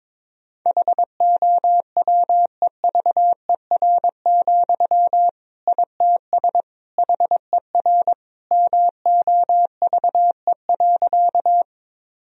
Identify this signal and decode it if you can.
Morse code, heard as a keyed tone, about 22 wpm, 700 Hz. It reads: HOWEVER, ITS HER MOVE.
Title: Sentences from Top 1000 Words - Review Part 1 - 22wpm